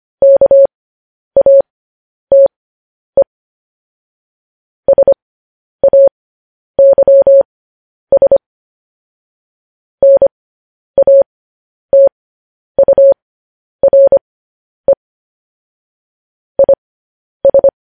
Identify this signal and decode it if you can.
Morse code, heard as a keyed tone, about 25 wpm, spaced as for 10 wpm, 550 Hz. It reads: KATE SAYS NATURE IS
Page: news-headlines-in-morse-code-at-10-wpm